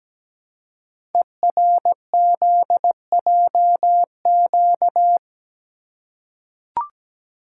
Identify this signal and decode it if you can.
Morse code, heard as a keyed tone, about 17 wpm, 700 Hz. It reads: ERZJQ
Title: Code Groups - 5 Letters - 17wpm